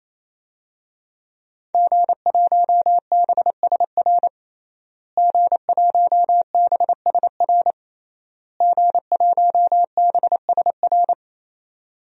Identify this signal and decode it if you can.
Morse code, heard as a keyed tone, about 27 wpm, 700 Hz. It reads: G1BSR G1BSR G1BSR
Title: DXCC-100 Call Signs - 5 Characters - Repeated 3x - 28wpm